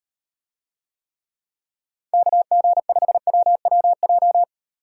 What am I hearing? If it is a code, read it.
Morse code, heard as a keyed tone, about 38 wpm, 700 Hz. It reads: KG5WWJ